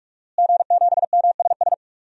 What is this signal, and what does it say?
Morse code, heard as a keyed tone, about 46 wpm, 700 Hz. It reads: G7MSS
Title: DXCC-100 Call Signs - 5 Characters - Repeated 2x - 45wpm